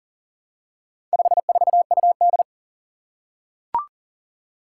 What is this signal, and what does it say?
Morse code, heard as a keyed tone, about 40 wpm, 700 Hz. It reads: 54UD
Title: Four Challenging Letter-Number Combinations 40wpm